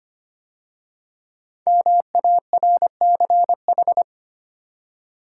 Morse code, heard as a keyed tone, about 25 wpm, 700 Hz. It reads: MARCH
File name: 5 Letter Words - First 3 Letters Given - 25wpm